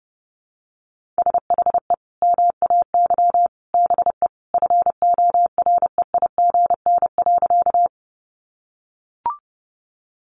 Morse code, heard as a keyed tone, about 30 wpm, 700 Hz. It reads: SHE MAY BE FOREIGN.
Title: Sentences from Top 1400 Words 30wpm